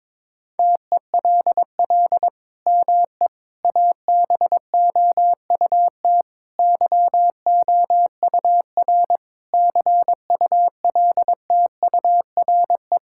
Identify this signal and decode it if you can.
Morse code, heard as a keyed tone, about 22 wpm, 700 Hz. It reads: TELL ME ABOUT YOUR CULTURE
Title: Sentences from Top 1000 Words - Review Part 6 - 22wpm